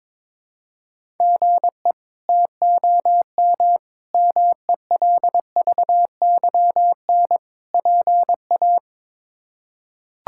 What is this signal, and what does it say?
Morse code, heard as a keyed tone, about 22 wpm, 700 Hz. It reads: GE TOM MELVYN PA
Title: K1USN SST QSO Practice - 22wpm